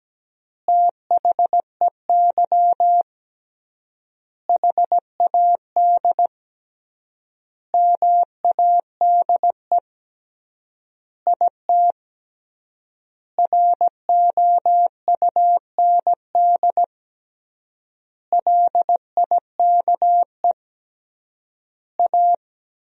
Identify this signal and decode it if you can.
Morse code, heard as a keyed tone, about 17 wpm, 700 Hz. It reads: THEY HAD MADE IT ROUND LIKE A
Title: 3x Word-Spacing - Sentences from Top 500 Words - 17wpm